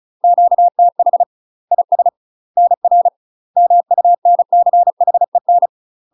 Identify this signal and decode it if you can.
Morse code, heard as a keyed tone, about 36 wpm, 700 Hz. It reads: QTH IS NR MUNCHEN
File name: UK MASTER QSOs TRAINING / 35WPM / UK MASTER QSOs 0031